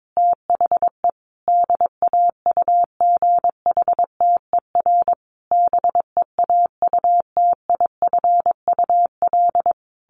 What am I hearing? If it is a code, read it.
Morse code, heard as a keyed tone, about 22 wpm, 700 Hz. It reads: THE DAUGHTER BEAUTIFUL
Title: Sentences from Top 1500 Words 22wpm